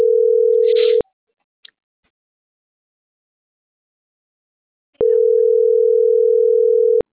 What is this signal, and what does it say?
Ringback tone (North America), 440 + 480 Hz.